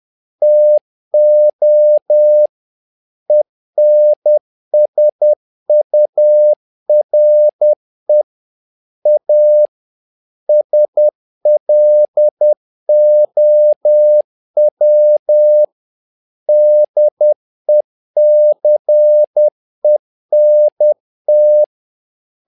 Morse code, heard as a keyed tone, about 10 wpm, 600 Hz. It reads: TO ENSURE A SLOW DECENT
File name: learning center / tower trouble / tower trouble 10wpm